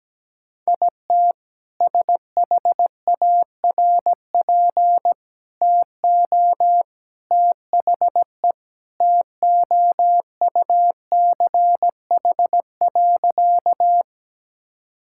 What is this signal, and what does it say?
Morse code, heard as a keyed tone, about 17 wpm, 700 Hz. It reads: IT SHARP TO THE TOUCH.
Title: Sentences from Top 1600 Words 17wpm